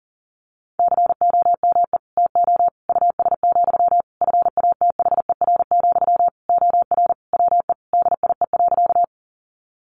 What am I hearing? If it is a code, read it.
Morse code, heard as a keyed tone, about 40 wpm, 700 Hz. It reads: COME TO US, FATHER, OR WE DIE.
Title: Sentences from Top 1300 Words 40wpm